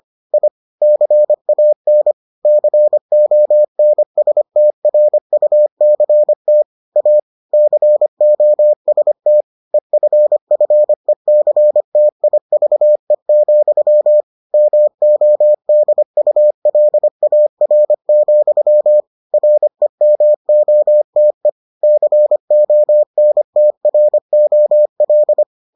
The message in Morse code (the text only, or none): I CAN CONSTRUCT A COST EFFECTIVE, MODULAR, REMOTE CONTROL